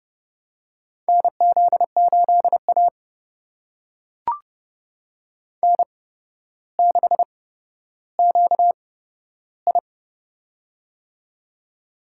Morse code, heard as a keyed tone, about 30 wpm, spaced as for 13 wpm, 700 Hz. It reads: NZ8A N6QI
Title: USA Call Signs - 4 Characters - 8x Character Spacing - 30wpm